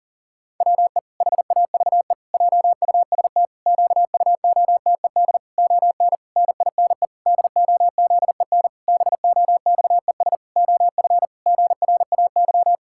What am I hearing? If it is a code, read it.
Morse code, heard as a keyed tone, about 40 wpm, 700 Hz. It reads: WE HAVE JUST QUOTED ON NINE DOZEN BOXES OF GRAY
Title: Pangram Challenge v2 - 40wpm